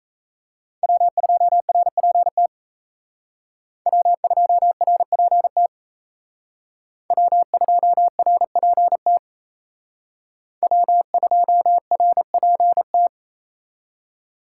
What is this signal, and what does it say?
Morse code, heard as a keyed tone, about 34 wpm, 700 Hz. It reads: W2RPT W2RPT W2RPT W2RPT
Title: Speed Racing - USA 1x3 Call Signs 28wpm